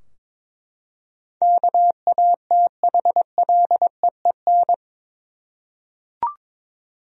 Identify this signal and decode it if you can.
Morse code, heard as a keyed tone, about 22 wpm, 700 Hz. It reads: KATHLEEN
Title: USA Common Names - 22wpm